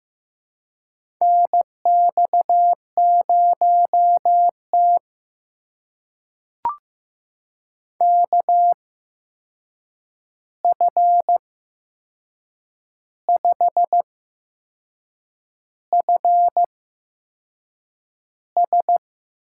Morse code, heard as a keyed tone, about 15 wpm, spaced as for 6 wpm, 700 Hz. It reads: NX0T KF5FS
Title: USA Call Signs - 8x Character Spacing - 15wpm